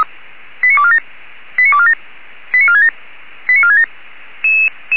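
Five-tone selective call (ZVEI), heard as quick runs of five tones, tones of about 70 ms. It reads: 89247 89247 89467 89467
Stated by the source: ZVEI five-tone selective call 89549 received by SDR at feuerwehr-orlamuende (fire brigade Orlamünde).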